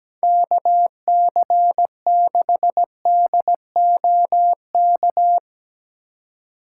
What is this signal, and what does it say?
Morse code, heard as a keyed tone, about 17 wpm, 700 Hz. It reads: KC6DOK
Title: USA Call Signs - 6 Characters - 3x Character Spacing - 17wpm